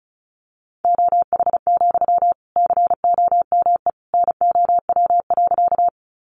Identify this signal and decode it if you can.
Morse code, heard as a keyed tone, about 34 wpm, 700 Hz. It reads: OH, COME NOW.
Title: Sentences from Top 500 Words 35wpm